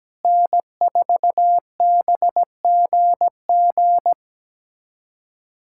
Morse code, heard as a keyed tone, about 17 wpm, 700 Hz. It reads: N4BGG
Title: USA Callsign DE Callsign - 5 Characters - 17wpm